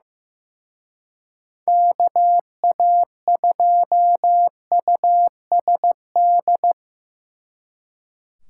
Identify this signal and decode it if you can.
Morse code, heard as a keyed tone, about 17 wpm, 700 Hz. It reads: KA2USD
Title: Speed Racing - USA 2x3 Call Signs 15wpm